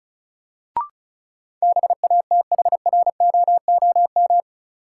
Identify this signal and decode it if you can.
Morse code, heard as a keyed tone, about 36 wpm, 700 Hz. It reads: BATHROOM
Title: Sets of 5 Words Encore - New Word Drill - 35wpm